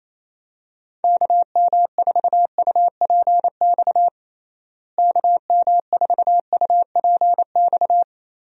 Morse code, heard as a keyed tone, about 28 wpm, 700 Hz. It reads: KM4UPX KM4UPX
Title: USA Call Signs - 6 Characters - Repeated 2x - 28wpm